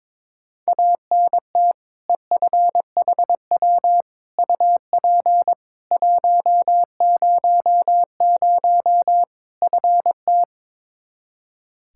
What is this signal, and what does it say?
Morse code, heard as a keyed tone, about 22 wpm, 700 Hz. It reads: ANT EFHW UP 100 FT
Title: Practice QSO - Exchange 2 - Info - 22wpm